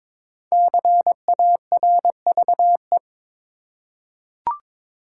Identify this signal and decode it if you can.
Morse code, heard as a keyed tone, about 22 wpm, 700 Hz. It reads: CARVE
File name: 5 Letter Words - First 3 Letters Given - 22wpm